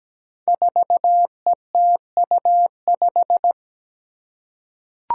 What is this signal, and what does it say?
Morse code, heard as a keyed tone, about 17 wpm, 700 Hz. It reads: 4ETU5
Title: Five Challenging Letter-Number Combinations 17wpm